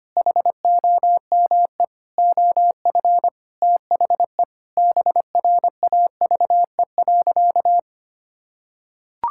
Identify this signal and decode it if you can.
Morse code, heard as a keyed tone, about 25 wpm, 700 Hz. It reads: HOME OF THE BRAVE.
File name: practice / 25wpm / Sentences with 9-Word Phrases - 25wpm